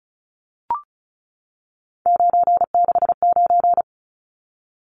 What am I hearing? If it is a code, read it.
Morse code, heard as a keyed tone, about 34 wpm, 700 Hz. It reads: 969
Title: Numbers - 3-Digit 35wpm